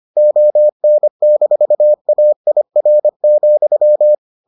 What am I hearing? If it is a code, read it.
Morse code, heard as a keyed tone, about 25 wpm, 600 Hz. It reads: ON-AIR,